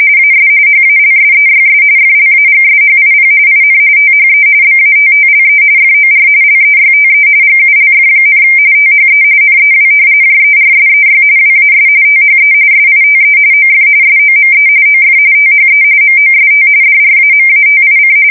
RTTY, about 45.45 baud. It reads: AS HALSEYS FLAGSHIP IN THE BATTLE FOR LEYTE GULF, THE NEW JERSEY WAS THE SITE OF SOME OF THE MOST DRAMATIC
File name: fsk170 / battleshipnj